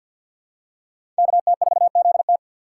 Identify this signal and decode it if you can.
Morse code, heard as a keyed tone, about 51 wpm, 700 Hz. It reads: KT4ZT
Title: USA Call Signs - 5 Characters - 6x Character Spacing - 50wpm